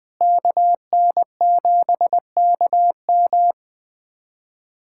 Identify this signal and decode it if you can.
Morse code, heard as a keyed tone, about 20 wpm, 700 Hz. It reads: KN7KM
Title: USA Callsign DE Callsign - 5 Characters - 20wpm